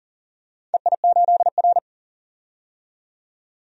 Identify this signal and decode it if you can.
Morse code, heard as a keyed tone, about 40 wpm, 700 Hz. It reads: EI8R
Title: DXCC-100 Call Signs - 4 Characters - 40wpm